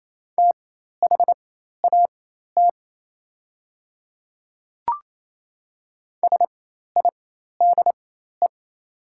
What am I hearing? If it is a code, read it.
Morse code, heard as a keyed tone, about 27 wpm, spaced as for 13 wpm, 700 Hz. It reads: THAT SIDE
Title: 4x Character Spacing - Top 100 Words - 28wpm